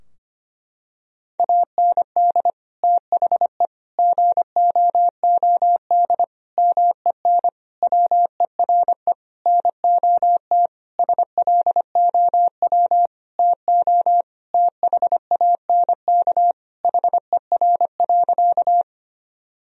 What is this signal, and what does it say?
Morse code, heard as a keyed tone, about 25 wpm, 700 Hz. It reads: AND THE GOOD MEN WERE NOT SLOW TO THANK HER.
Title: Sentences from Top 1000 Words - Review Part 2 - 25wpm